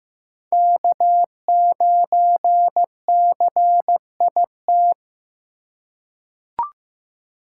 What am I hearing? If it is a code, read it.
Morse code, heard as a keyed tone, about 15 wpm, 700 Hz. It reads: K9CIT
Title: USA Call Signs - 5 Characters - 4x Character Spacing - 15wpm